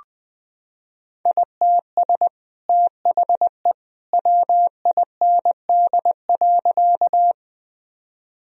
Morse code, heard as a keyed tone, about 20 wpm, 700 Hz. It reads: ITS THE WIND.
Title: Sentences from Top 1000 Words - Review Part 2 - 20wpm